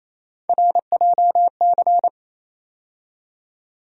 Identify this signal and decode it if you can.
Morse code, heard as a keyed tone, about 28 wpm, 700 Hz. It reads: RJC
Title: USA Call Sign Postfixes - 3 Letters 28wpm